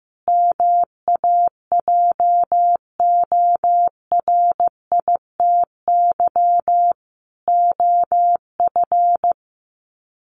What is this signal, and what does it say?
Morse code, heard as a keyed tone, about 15 wpm, 700 Hz. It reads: MAJORITY OF
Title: Sets of 4 Words 15wpm